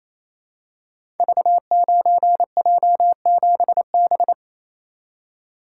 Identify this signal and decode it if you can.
Morse code, heard as a keyed tone, about 28 wpm, 700 Hz. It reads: V9J7B
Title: Code Groups - 5 Letter:Numbers - 28wpm